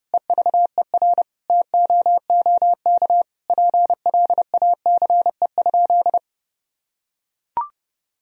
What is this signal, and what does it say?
Morse code, heard as a keyed tone, about 30 wpm, 700 Hz. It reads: EVER TOOK PLACE?
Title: Sentences from Top 1000 Words - Review Part 1 - 30wpm